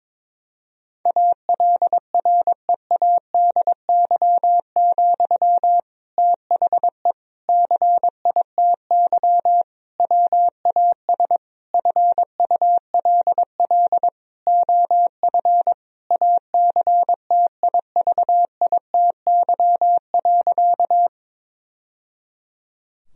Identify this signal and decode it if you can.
Morse code, heard as a keyed tone, about 21 wpm, 700 Hz. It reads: ALREADY, THE CITY WAS FULL OF ACTIVITY.
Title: Sentences from Top 1000 Words - Review Part 4 - 22wpm